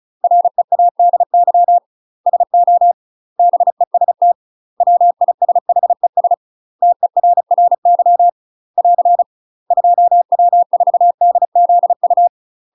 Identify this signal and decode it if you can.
Morse code, heard as a keyed tone, about 36 wpm, 700 Hz. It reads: READY SO BEST WISHES TERRY <AR> 2W4DZU
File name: UK MASTER QSOs TRAINING / 35WPM / UK MASTER QSOs 0002